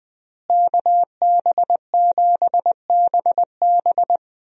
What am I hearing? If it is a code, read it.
Morse code, heard as a keyed tone, about 20 wpm, 700 Hz. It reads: KB7BB
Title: USA Call Signs - 5 Characters - Repeated 2x - 20wpm